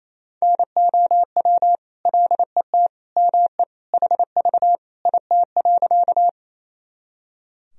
Morse code, heard as a keyed tone, about 27 wpm, 700 Hz. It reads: NOW LET ME HV IT.
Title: Sentences from Top 300 Words and Abbreviations 28wpm